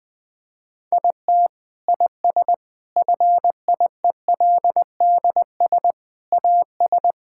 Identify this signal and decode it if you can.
Morse code, heard as a keyed tone, about 21 wpm, 700 Hz. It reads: IT IS FIELDS AS
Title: Sentences with 7-Word Phrases - 20wpm